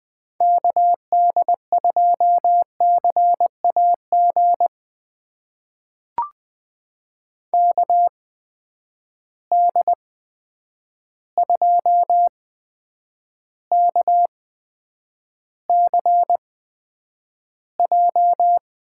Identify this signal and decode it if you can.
Morse code, heard as a keyed tone, about 20 wpm, spaced as for 9 wpm, 700 Hz. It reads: KD2CAG KD2KCJ
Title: USA Call Signs - 6 Characters - 8x Character Spacing - 20wpm